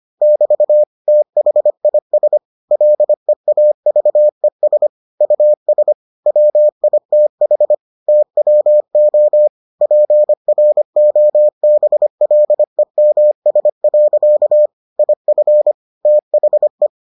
Morse code, heard as a keyed tone, about 25 wpm, 600 Hz. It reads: <BT> THIS LEAVES US WITH TWO PROBLEMS. IF THE